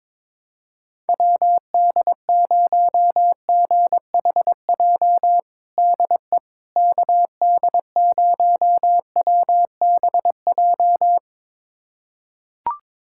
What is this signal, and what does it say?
Morse code, heard as a keyed tone, about 22 wpm, 700 Hz. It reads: WD0GHJ DE KD0WBJ
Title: USA Callsign DE Callsign - 6 Characters - 22wpm